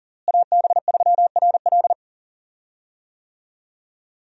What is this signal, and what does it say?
Morse code, heard as a keyed tone, about 40 wpm, 700 Hz. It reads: AB3RL
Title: USA Call Signs - 5 Characters - Repeated 3x - 40wpm